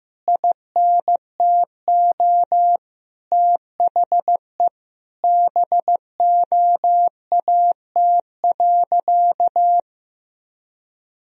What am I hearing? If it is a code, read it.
Morse code, heard as a keyed tone, about 15 wpm, 700 Hz. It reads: INTO THE BOAT.
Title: Sentences from Top 1000 Words - Review Part 3 - 15wpm